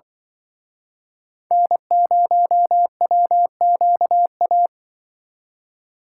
Morse code, heard as a keyed tone, about 25 wpm, 700 Hz. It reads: N0WQA
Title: Speed Racing - USA 1x3 Call Signs 20wpm